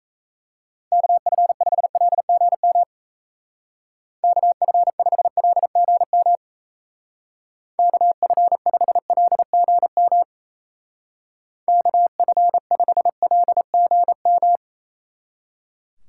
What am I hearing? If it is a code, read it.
Morse code, heard as a keyed tone, about 34 wpm, 700 Hz. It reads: KF5LGM KF5LGM KF5LGM KF5LGM
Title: Speed Racing - USA 2x3 Call Signs 28wpm